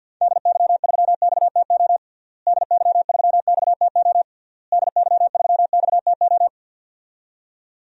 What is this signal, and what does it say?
Morse code, heard as a keyed tone, about 48 wpm, 700 Hz. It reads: DY3XTO DY3XTO DY3XTO
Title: DXCC-100 Call Signs - 6 Characters - Repeated 3x - 50wpm